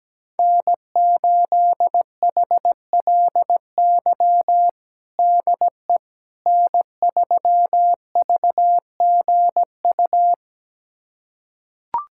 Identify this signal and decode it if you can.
Morse code, heard as a keyed tone, about 17 wpm, 700 Hz. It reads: N8HLY DE N3VGU
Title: USA Callsign DE Callsign - 5 Characters - 17wpm